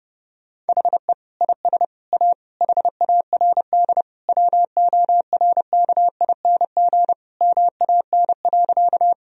Morse code, heard as a keyed tone, about 32 wpm, 700 Hz. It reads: HE IS A HARD WORKING MAN.
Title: Sentences from Top 1000 Words - Review Part 4 - 30wpm